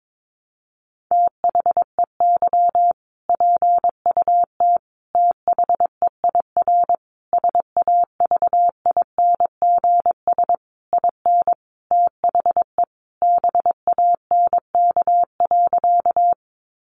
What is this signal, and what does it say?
Morse code, heard as a keyed tone, about 22 wpm, 700 Hz. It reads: THEY PUT THEIR SAVINGS IN THE BANK.
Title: Sentences from Top 1200 Words 22wpm